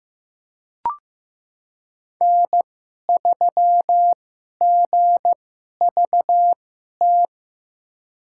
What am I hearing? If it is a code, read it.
Morse code, heard as a keyed tone, about 15 wpm, spaced as for 11 wpm, 700 Hz. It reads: N3GVT
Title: USA Call Signs - 5 Characters - 2x Character Spacing - 15wpm